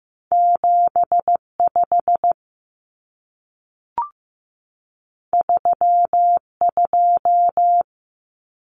Morse code, heard as a keyed tone, about 15 wpm, 700 Hz. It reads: 75 32
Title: Common Cut Numbers - 2-Digit 15wpm